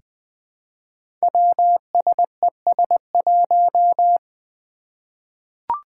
Morse code, heard as a keyed tone, about 20 wpm, 700 Hz. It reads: WSES1
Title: Five Challenging Letter-Number Combinations 20wpm